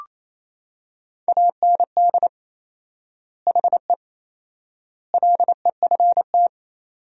Morse code, heard as a keyed tone, about 28 wpm, 700 Hz. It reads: AND HE LEFT
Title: 4x Word-Spacing - Sentences from Top 300 Words - 28wpm